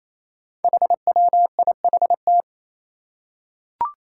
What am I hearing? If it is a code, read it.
Morse code, heard as a keyed tone, about 27 wpm, 700 Hz. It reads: HWIHT
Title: Five Challenging Letter-Number Combinations 28wpm